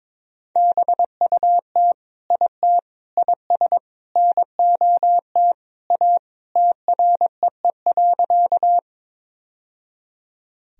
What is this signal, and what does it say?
Morse code, heard as a keyed tone, about 22 wpm, 700 Hz. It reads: BUT IT IS NOT A TREE.
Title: Sentences from Top 1000 Words - Review Part 1 - 22wpm